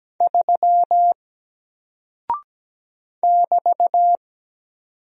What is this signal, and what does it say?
Morse code, heard as a keyed tone, about 17 wpm, 700 Hz. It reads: 3 <BT>
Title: Single Character (Letter:Number:Punctuation:Prosign) Alt 17wpm